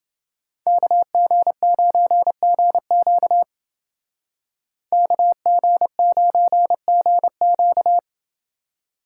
Morse code, heard as a keyed tone, about 28 wpm, 700 Hz. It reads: KG9GQ KG9GQ